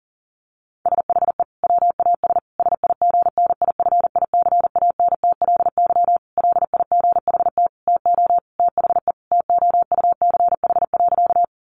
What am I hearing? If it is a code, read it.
Morse code, heard as a keyed tone, about 40 wpm, 700 Hz. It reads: SHE WAS SIGNIFICANTLY LIGHT TO THE TOUCH.
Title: Sentences from Top 1500 Words 40wpm